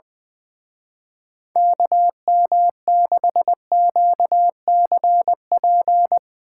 Morse code, heard as a keyed tone, about 21 wpm, 700 Hz. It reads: KM6QCP
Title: Speed Racing - USA 2x3 Call Signs 20wpm